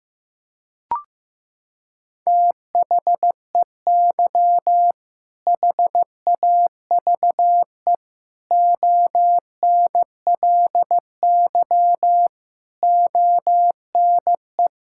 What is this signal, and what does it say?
Morse code, heard as a keyed tone, about 15 wpm, 700 Hz. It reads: THEY HAVE ONLY ONE
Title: Sentences from Top 1600 Words 15wpm